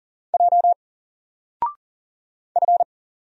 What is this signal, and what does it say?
Morse code, heard as a keyed tone, about 38 wpm, 700 Hz. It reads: J F
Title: Single Character (Letter:Number:Punctuation:Prosign) 40wpm